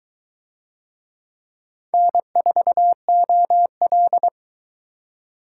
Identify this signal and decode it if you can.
Morse code, heard as a keyed tone, about 21 wpm, 700 Hz. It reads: N4OL